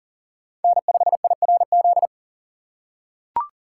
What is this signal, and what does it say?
Morse code, heard as a keyed tone, about 40 wpm, 700 Hz. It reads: N5IRZ
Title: USA Call Signs - 4x Character Spacing - 40wpm